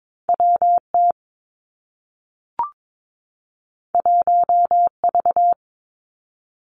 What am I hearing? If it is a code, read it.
Morse code, heard as a keyed tone, about 22 wpm, 700 Hz. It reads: WT 1V
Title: Two Challenging Letter-Number Combinations 22wpm